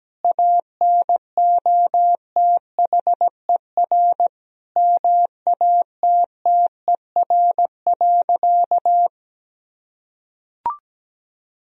Morse code, heard as a keyed tone, about 17 wpm, 700 Hz. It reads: ANOTHER MATTER.
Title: Sentences from Top 1000 Words - Review Part 1 - 17wpm